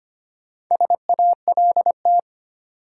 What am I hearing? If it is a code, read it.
Morse code, heard as a keyed tone, about 25 wpm, 700 Hz. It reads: SALT
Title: Sets of 5 Words - New Word Drill - 25wpm